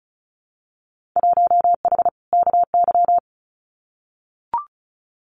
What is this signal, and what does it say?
Morse code, heard as a keyed tone, about 34 wpm, 700 Hz. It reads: 1H KY
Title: Winter Field Day Exchanges 35wpm